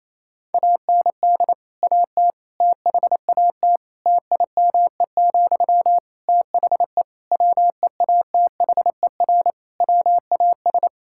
Morse code, heard as a keyed tone, about 28 wpm, 700 Hz. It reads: AND AT THAT TIME, THE WEATHER WAS
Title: Sentences from Top 1000 Words - Review Part 5 - 28wpm